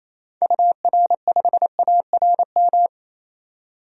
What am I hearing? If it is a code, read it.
Morse code, heard as a keyed tone, about 28 wpm, 700 Hz. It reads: UR5ARM
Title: DXCC-100 Call Signs - 6 Characters - 28wpm